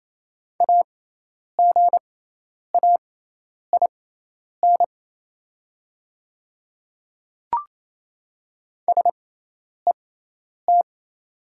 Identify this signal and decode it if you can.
Morse code, heard as a keyed tone, about 27 wpm, spaced as for 10 wpm, 700 Hz. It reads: AGAIN SET
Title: 6x Character Spacing - Top 200 Words - 28wpm